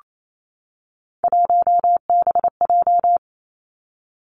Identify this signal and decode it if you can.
Morse code, heard as a keyed tone, about 28 wpm, 700 Hz. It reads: 1BJ
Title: Three Challenging Letter-Number Combinations 28wpm